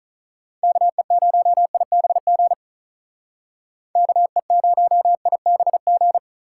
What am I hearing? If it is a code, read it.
Morse code, heard as a keyed tone, about 45 wpm, 700 Hz. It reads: KE0IBG KE0IBG